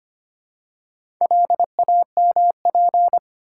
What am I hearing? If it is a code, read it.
Morse code, heard as a keyed tone, about 25 wpm, 700 Hz. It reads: LAMP
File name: practice / 25wpm / Easy Words with Hints - 25wpm